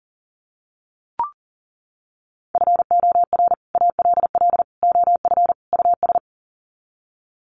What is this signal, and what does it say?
Morse code, heard as a keyed tone, about 40 wpm, 700 Hz. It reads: FOR ALL OF US
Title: Sets of 4 Words 40wpm